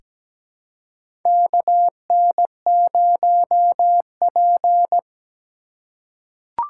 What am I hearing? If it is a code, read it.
Morse code, heard as a keyed tone, about 17 wpm, 700 Hz. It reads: KN0P
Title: USA Call Signs - 4 Characters - 2x Character Spacing - 17wpm